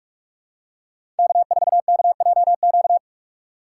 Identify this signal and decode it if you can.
Morse code, heard as a keyed tone, about 46 wpm, 700 Hz. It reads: K4KJQ
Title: USA Call Signs - 5 Characters - 3x Character Spacing - 45wpm